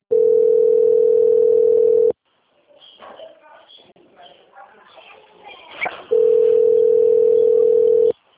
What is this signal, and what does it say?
Ringback tone (North America), 440 + 480 Hz, with speech.
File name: out- - - - - .47787